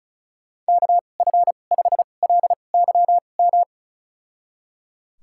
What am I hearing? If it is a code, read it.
Morse code, heard as a keyed tone, about 36 wpm, spaced as for 25 wpm, 690 Hz. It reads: KF5LYM
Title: USA Call Signs - 6 Characters - 2x Character Spacing - 35wpm